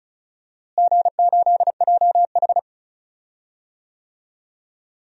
Morse code, heard as a keyed tone, about 36 wpm, 700 Hz. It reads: G8JH
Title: DXCC-100 Call Signs - 4 Characters - Repeated 3x - 35wpm